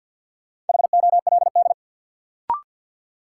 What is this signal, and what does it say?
Morse code, heard as a keyed tone, about 51 wpm, 700 Hz. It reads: HOLD
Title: Sentences from Top 400 Words - New Word Drill - 50wpm